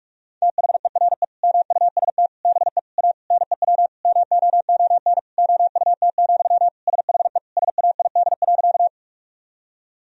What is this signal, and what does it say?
Morse code, heard as a keyed tone, about 47 wpm, 700 Hz. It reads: THERE MUST BE A NEW MOON OUT, SHE SAID.
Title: Sentences from Top 1000 Words - Review Part 3 - 45wpm